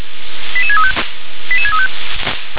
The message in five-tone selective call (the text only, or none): 88635 88635